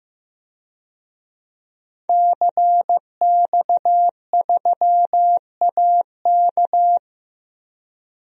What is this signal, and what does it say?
Morse code, heard as a keyed tone, about 15 wpm, 700 Hz. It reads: CX3AK
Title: DXCC-100 Call Signs - 5 Characters - Repeated 2x - 15wpm